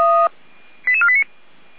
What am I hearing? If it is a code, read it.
Five-tone selective call (ZVEI), heard as quick runs of five tones, tones of about 70 ms. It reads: 88489
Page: /455/SDR/ZVEI/88216/